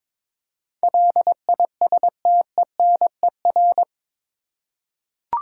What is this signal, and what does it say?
Morse code, heard as a keyed tone, about 22 wpm, 700 Hz. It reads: LISTENER
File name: practice / 22wpm / Eight Letter Words - 22wpm